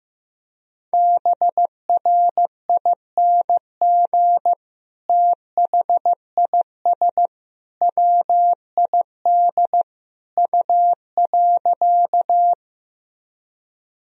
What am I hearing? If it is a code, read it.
Morse code, heard as a keyed tone, about 15 wpm, 700 Hz. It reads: BRING THIS WID U.
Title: Sentences from Top 500 Words and Abbreviations 15wpm